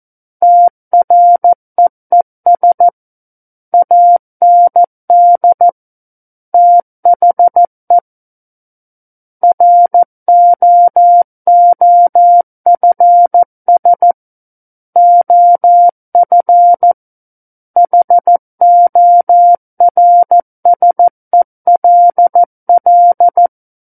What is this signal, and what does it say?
Morse code, heard as a keyed tone, about 14 wpm, 700 Hz. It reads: TREES AND THE ROOFS OF HORSELL
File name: Book One / 05-Chapter 5 - 14 WPM